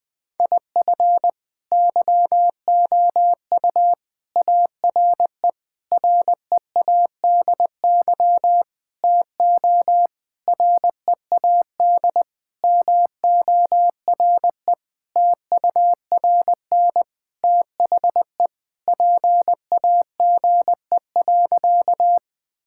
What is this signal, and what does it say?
Morse code, heard as a keyed tone, about 21 wpm, 700 Hz. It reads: IF YOU ARE READY TO READ MORE TURN THE PAGE.
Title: Sentences with 6-Word Phrases - Spoken Beginning - 20wpm